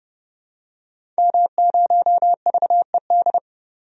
Morse code, heard as a keyed tone, about 32 wpm, 700 Hz. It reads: M0VED
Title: DXCC-100 Call Signs - 5 Characters - 30wpm